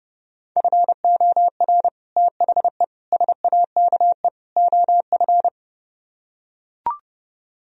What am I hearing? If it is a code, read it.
Morse code, heard as a keyed tone, about 30 wpm, 700 Hz. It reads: FOR THE SAKE OF